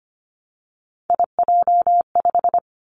Morse code, heard as a keyed tone, about 25 wpm, 700 Hz. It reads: IJ5
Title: Three Challenging Letter-Number Combinations 25wpm